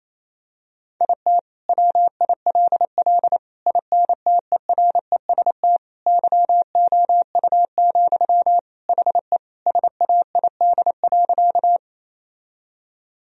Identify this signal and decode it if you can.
Morse code, heard as a keyed tone, about 27 wpm, 700 Hz. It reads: IT WILL INTEREST YOU, HE SAID.
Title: Sentences from Top 1000 Words - Review Part 2 - 28wpm